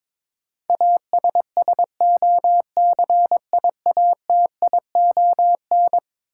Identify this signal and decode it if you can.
Morse code, heard as a keyed tone, about 22 wpm, 700 Hz. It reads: ASSOCIATION
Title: Sentences from Top 1100 Words - New Word Drill Alt - 22wpm